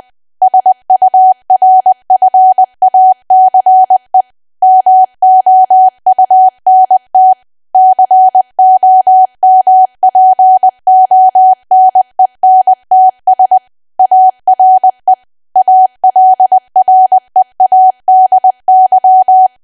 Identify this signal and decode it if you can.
Morse code, heard as a keyed tone, about 20 wpm, 750 Hz. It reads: SURFACE MOUNT COMPONENTS ARE ALREADY
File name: cwa / W1AW-Level3 / 20WPM